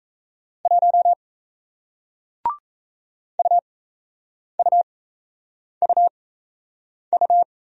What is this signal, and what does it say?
Morse code, heard as a keyed tone, about 37 wpm, spaced as for 8 wpm, 700 Hz. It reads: 1 UUUU